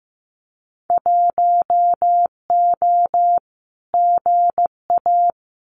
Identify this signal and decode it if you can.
Morse code, heard as a keyed tone, about 15 wpm, 700 Hz. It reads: 1O GA
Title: Winter Field Day Exchanges 15wpm